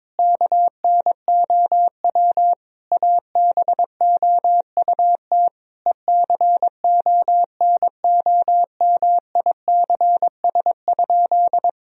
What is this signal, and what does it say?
Morse code, heard as a keyed tone, about 21 wpm, 700 Hz. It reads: KNOW ABOUT ECONOMICS?
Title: Sentences from Top 1000 Words - Review Part 3 - 22wpm